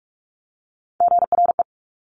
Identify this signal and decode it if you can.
Morse code, heard as a keyed tone, about 46 wpm, 700 Hz. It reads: GRE
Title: Top Three Letter Combinations in Words 45wpm